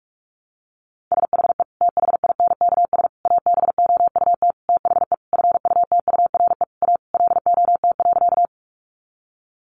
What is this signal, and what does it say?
Morse code, heard as a keyed tone, about 44 wpm, 700 Hz. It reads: SHE THINKS ABOUT THE FUTURE A LOT.
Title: Easy Phrases 45wpm